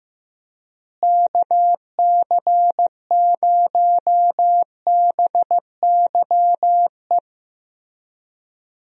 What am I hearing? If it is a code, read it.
Morse code, heard as a keyed tone, about 15 wpm, 700 Hz. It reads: KC0BYE
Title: USA Call Signs - 6 Characters - Repeated 2x - 15wpm